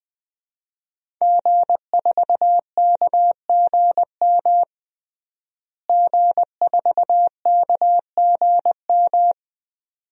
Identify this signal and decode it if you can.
Morse code, heard as a keyed tone, about 20 wpm, 700 Hz. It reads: G4KGM G4KGM
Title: DXCC-100 Call Signs - 5 Characters - Repeated 2x - 20wpm